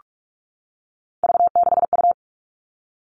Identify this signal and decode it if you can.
Morse code, heard as a keyed tone, about 46 wpm, 700 Hz. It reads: V6U
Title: Three Challenging Letter-Number Combinations 45wpm